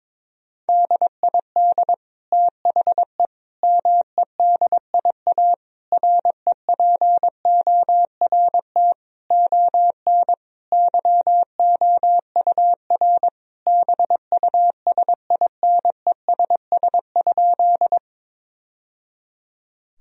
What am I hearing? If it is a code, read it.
Morse code, heard as a keyed tone, about 22 wpm, 700 Hz. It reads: DID THE MEDIA REPORT ON YOUR BUSINESS?
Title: Sentences from Top 1000 Words - Review Part 4 - 22wpm